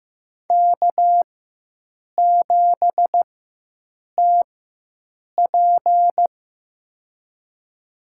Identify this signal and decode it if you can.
Morse code, heard as a keyed tone, about 15 wpm, spaced as for 7 wpm, 700 Hz. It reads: K7TP
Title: USA Call Signs - 4 Characters - 4x Character Spacing - 15wpm